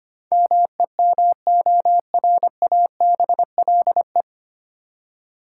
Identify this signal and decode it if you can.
Morse code, heard as a keyed tone, about 25 wpm, 700 Hz. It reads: MEMORABLE
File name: Nine Letter Words - 25wpm